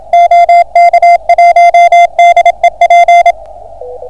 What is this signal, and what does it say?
Morse code, heard as a keyed tone, about 27 wpm, 690 Hz. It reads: OK1DEP